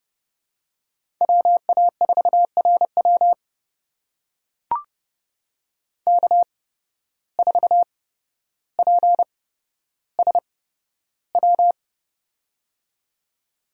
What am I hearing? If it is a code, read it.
Morse code, heard as a keyed tone, about 31 wpm, spaced as for 13 wpm, 700 Hz. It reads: WA4RW K4PSW
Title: USA Call Signs - 5 Characters - 8x Character Spacing - 30wpm